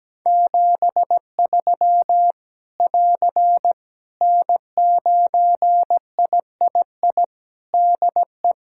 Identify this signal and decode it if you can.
Morse code, heard as a keyed tone, about 17 wpm, 700 Hz. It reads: 73 <AR> N9III DE
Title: Practice QSO - Exchange 4 - Ending - 17wpm